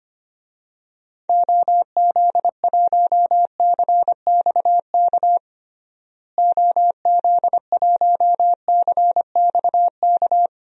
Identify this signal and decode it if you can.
Morse code, heard as a keyed tone, about 25 wpm, 700 Hz. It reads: OZ1CXK OZ1CXK
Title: DXCC-100 Call Signs - 6 Characters - Repeated 3x - 25wpm